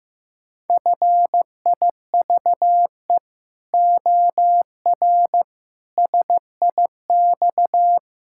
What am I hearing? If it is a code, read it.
Morse code, heard as a keyed tone, about 15 wpm, 700 Hz. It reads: FIVE OR SIX
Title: Pangram Challenge v2 - 15wpm